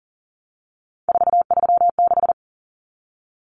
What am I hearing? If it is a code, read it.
Morse code, heard as a keyed tone, about 40 wpm, 700 Hz. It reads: 436
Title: Numbers - 3-Digit 40wpm